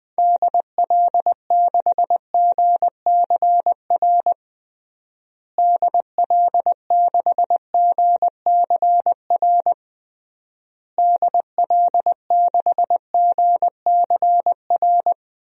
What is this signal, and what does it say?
Morse code, heard as a keyed tone, about 21 wpm, 700 Hz. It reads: DL6GCR DL6GCR DL6GCR
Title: DXCC-100 Call Signs - 6 Characters - Repeated 3x - 20wpm